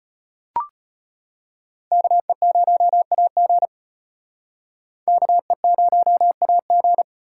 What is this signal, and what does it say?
Morse code, heard as a keyed tone, about 31 wpm, 700 Hz. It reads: KE0AG KE0AG
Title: Speed Racing - USA 2x2 Call Signs 25wpm